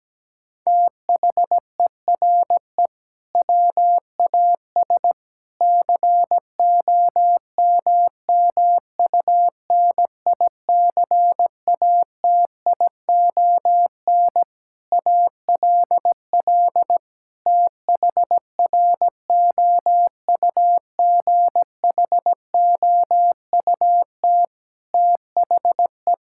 Morse code, heard as a keyed tone, about 17 wpm, 700 Hz. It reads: THERE WAS COMMUNICATION ALL THROUGHOUT THE
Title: Sentences from Top 1000 Words - Review Part 6 - 17wpm